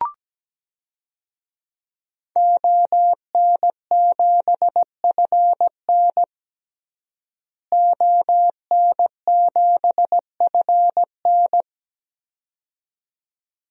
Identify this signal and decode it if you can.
Morse code, heard as a keyed tone, about 17 wpm, 700 Hz. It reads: ON7FN ON7FN
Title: DXCC-100 Call Signs - 5 Characters - Repeated 2x - 17wpm